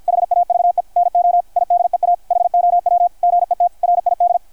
Morse code, heard as a keyed tone, about 53 wpm, 690 Hz, with radio noise. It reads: HAVE NO IDEA HOW GET RID
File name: archived / CRSnet